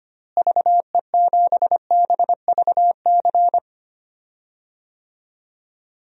Morse code, heard as a keyed tone, about 25 wpm, 700 Hz. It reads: VE7BVC
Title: DXCC-100 Call Signs - 6 Characters - Repeated 2x - 25wpm